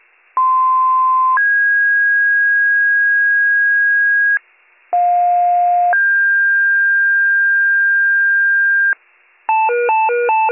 Two-tone page, heard as a one-second tone, then a three-second tone, about 1040 then 1650 Hz, about 710 then 1600 Hz. It is heard from 0.4 to 8.9 s.